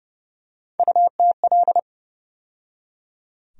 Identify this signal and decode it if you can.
Morse code, heard as a keyed tone, about 30 wpm, 700 Hz. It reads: UTL
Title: USA Call Sign Postfixes - 3 Letters 30wpm